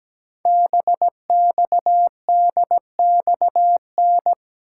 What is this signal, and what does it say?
Morse code, heard as a keyed tone, about 17 wpm, 700 Hz. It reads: BXDXN
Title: Code Groups - 5 Letter:Numbers - 17wpm